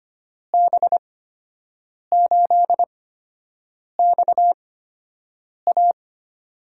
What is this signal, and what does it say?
Morse code, heard as a keyed tone, about 25 wpm, spaced as for 7 wpm, 700 Hz. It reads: B8XA
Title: USA Call Signs - 8x Character Spacing - 25wpm